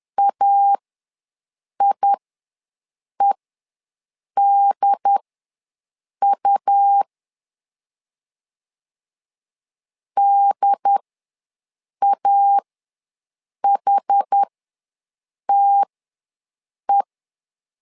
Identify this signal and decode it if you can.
Morse code, heard as a keyed tone, about 10 wpm, spaced as for 6 wpm, 800 Hz. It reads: AIEDU DAHTE
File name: LEZIONI / 2 UHDN / 2 UHD ws 10wpm 2058